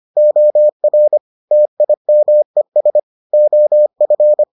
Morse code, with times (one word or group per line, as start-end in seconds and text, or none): 0.2-1.2 OR
1.5-3.0 TIMES
3.3-4.4 OF